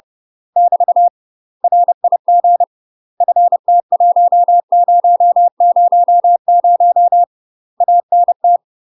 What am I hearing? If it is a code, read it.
Morse code, heard as a keyed tone, about 30 wpm, 700 Hz. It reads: <BT> RIG FT1000 ANT
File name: UK MASTER QSOs TRAINING / 30WPM / UK MASTER QSOs 0041